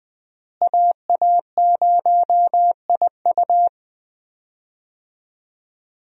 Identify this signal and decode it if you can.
Morse code, heard as a keyed tone, about 20 wpm, 700 Hz. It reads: AA0IU
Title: USA Call Signs - 5 Characters - Repeated 2x - 20wpm